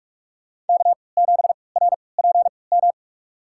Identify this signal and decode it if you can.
Morse code, heard as a keyed tone, about 47 wpm, spaced as for 26 wpm, 700 Hz. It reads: K7RPM
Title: USA Call Signs - 5 Characters - 3x Character Spacing - 45wpm